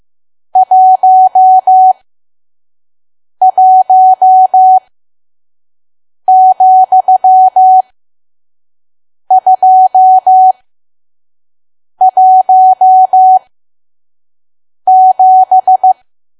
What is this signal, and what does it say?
Morse code, heard as a keyed tone, about 15 wpm, spaced as for 5 wpm, 750 Hz. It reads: 11,217
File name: Archive / 5 WPM files / PM